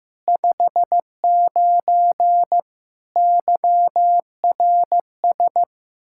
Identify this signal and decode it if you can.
Morse code, heard as a keyed tone, about 16 wpm, 700 Hz. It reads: 59 YRS
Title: Practice QSO - Exchange 3 - More Info - 15wpm